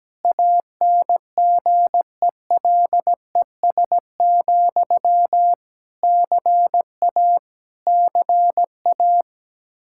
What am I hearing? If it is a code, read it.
Morse code, heard as a keyed tone, about 17 wpm, 700 Hz. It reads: ANGELES, CA CA
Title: Practice QSO - Exchange 1 - Beginning - 17wpm